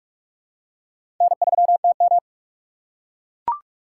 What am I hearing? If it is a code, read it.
Morse code, heard as a keyed tone, about 44 wpm, 700 Hz. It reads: N3TM
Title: USA Call Signs - 4 Characters - 3x Character Spacing - 45wpm